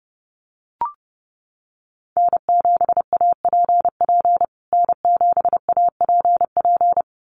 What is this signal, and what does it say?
Morse code, heard as a keyed tone, about 30 wpm, 700 Hz. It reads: N7APP N7APP
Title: USA Call Signs Repeated 2x 30wpm